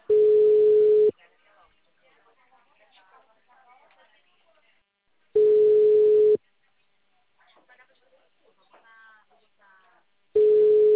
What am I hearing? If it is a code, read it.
Ringback tone (Europe, 425 Hz), with speech.